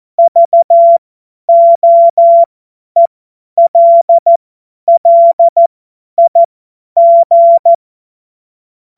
Morse code, heard as a keyed tone, about 14 wpm, spaced as for 10 wpm, 680 Hz. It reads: VOELLIG
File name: Morsekurs / Witze / Witz 22. Juni 2021, 50 F70